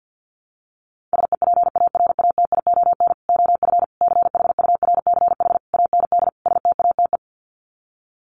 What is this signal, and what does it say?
Morse code, heard as a keyed tone, about 50 wpm, 700 Hz. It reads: SEPARATION OF CHURCH AND STATE
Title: Sets of 5 Words 50wpm